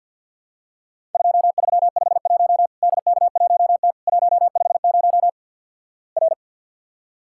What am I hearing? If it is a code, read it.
Morse code, heard as a keyed tone, about 50 wpm, 700 Hz. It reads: 2351 DK1T 150 R
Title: Worked All Europe DX Contest - QTC Exchange with Repeats - 50wpm